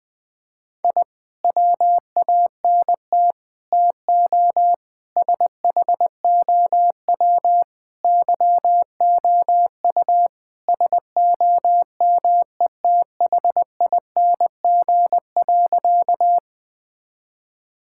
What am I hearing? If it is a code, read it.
Morse code, heard as a keyed tone, about 20 wpm, 700 Hz. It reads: I WANT TO SHOW YOU SOMETHING.
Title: Sentences with 5-Word Phrases - Spoken Beginning - 20wpm